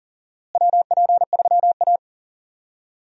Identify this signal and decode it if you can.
Morse code, heard as a keyed tone, about 40 wpm, 700 Hz. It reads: WP3A
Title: USA Call Signs - 4 Characters - Repeated 2x - 40wpm